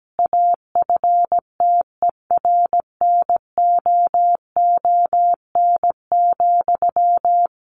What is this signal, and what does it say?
Morse code, heard as a keyed tone, about 17 wpm, 700 Hz. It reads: AFTERNOON,
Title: Sentences from Top 1500 Words 17wpm